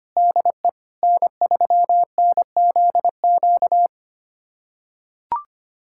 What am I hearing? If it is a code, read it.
Morse code, heard as a keyed tone, about 25 wpm, 700 Hz. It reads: DE N3NZQ
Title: USA Callsign DE Callsign 25wpm